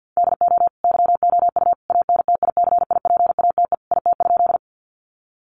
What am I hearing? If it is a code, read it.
Morse code, heard as a keyed tone, about 50 wpm, 700 Hz. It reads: DO YOU ANTICIPATE IT?
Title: Sentences from Top 1500 Words 50wpm